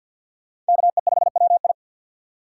Morse code, heard as a keyed tone, about 50 wpm, 700 Hz. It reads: KE5WI
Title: USA Call Signs - 5 Characters - 4x Character Spacing - 50wpm